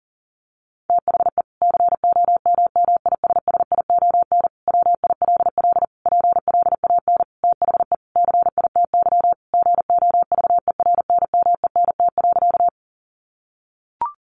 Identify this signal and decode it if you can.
Morse code, heard as a keyed tone, about 40 wpm, 700 Hz. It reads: THE COMMISSION WILL PLAN THE CITY GOVERNMENT.
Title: Sentences from Top 1400 Words 40wpm